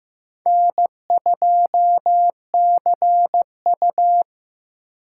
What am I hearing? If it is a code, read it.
Morse code, heard as a keyed tone, about 17 wpm, 700 Hz. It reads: N2CU
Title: Speed Racing - USA 1x2 Call Signs 15wpm